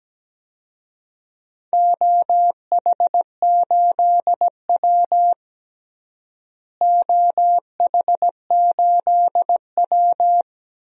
Morse code, heard as a keyed tone, about 17 wpm, 700 Hz. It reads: OH8W OH8W
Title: DXCC-100 Call Signs - 4 Characters - Repeated 2x - 17wpm